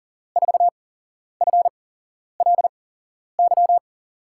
Morse code, heard as a keyed tone, about 40 wpm, spaced as for 11 wpm, 700 Hz. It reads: 4FLY
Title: USA Call Signs - 6 Characters - 8x Character Spacing - 40wpm